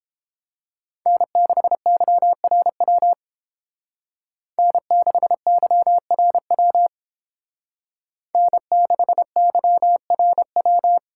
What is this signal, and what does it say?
Morse code, heard as a keyed tone, about 30 wpm, 700 Hz. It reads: N6YRW N6YRW N6YRW